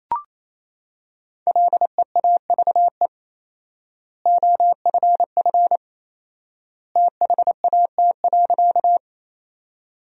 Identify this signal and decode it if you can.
Morse code, heard as a keyed tone, about 28 wpm, 700 Hz. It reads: LEAVE OFF THAT.
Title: 4x Word-Spacing - Sentences from Top 400 Words - 28wpm